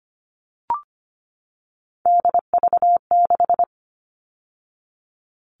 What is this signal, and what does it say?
Morse code, heard as a keyed tone, about 25 wpm, 700 Hz. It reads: DV6
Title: All Cut Numbers - 3-Digit 25wpm